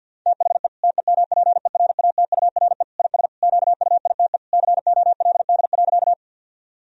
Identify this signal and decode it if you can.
Morse code, heard as a keyed tone, about 50 wpm, 700 Hz. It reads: THE TEMPERATURE IS QUITE COLD.
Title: Sentences from Top 1000 Words - Review Part 4 - 50wpm